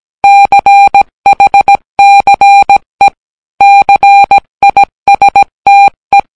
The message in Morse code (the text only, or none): CHCE CISTE